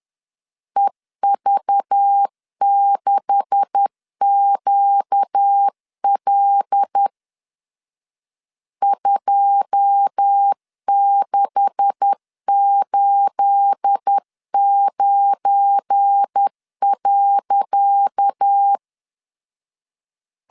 Morse code, heard as a keyed tone, about 10 wpm, 800 Hz. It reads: EV6QL 2689.
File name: LEZIONI / 10 SIGNS / 10C DOT 10wpm 2940